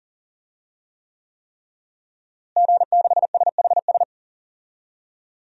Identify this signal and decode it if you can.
Morse code, heard as a keyed tone, about 40 wpm, 700 Hz. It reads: G6SHS
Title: DXCC-100 Call Signs - 5 Characters - Repeated 3x - 40wpm